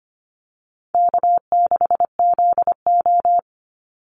Morse code, heard as a keyed tone, about 25 wpm, 700 Hz. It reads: K6ZO
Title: Easy Call Signs Repeated 3x - 25wpm